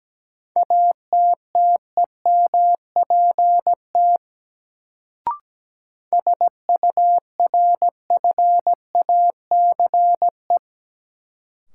Morse code, heard as a keyed tone, about 17 wpm, 700 Hz. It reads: ATTEMPT SURFACE
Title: Sets of 4 Words - New Word Drill - 17wpm